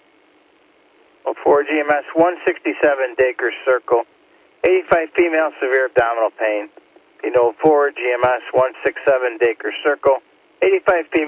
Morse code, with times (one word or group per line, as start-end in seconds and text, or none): none